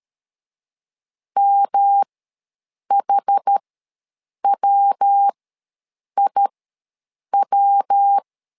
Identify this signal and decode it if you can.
Morse code, heard as a keyed tone, about 13 wpm, spaced as for 7 wpm, 800 Hz. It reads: MHWIW
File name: LEZIONI / 3 MWRO / 3 MW ws 12wpm 1816